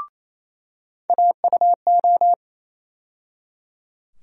Morse code, heard as a keyed tone, about 28 wpm, 700 Hz. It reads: AUO
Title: USA Call Sign Postfixes - 3 Letters 28wpm